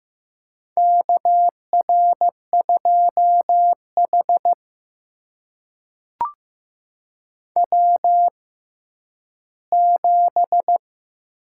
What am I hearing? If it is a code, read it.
Morse code, heard as a keyed tone, about 15 wpm, spaced as for 10 wpm, 700 Hz. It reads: KR2H W7
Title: USA Call Signs - 4 Characters - 6x Character Spacing - 15wpm